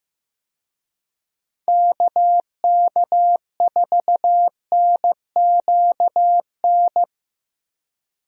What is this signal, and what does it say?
Morse code, heard as a keyed tone, about 15 wpm, 700 Hz. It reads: KK4NQN
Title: USA Call Signs - 6 Characters - Repeated 3x - 15wpm